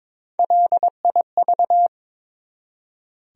Morse code, heard as a keyed tone, about 22 wpm, 700 Hz. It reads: LIV
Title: USA Common Names - Max 3 Letters - 22wpm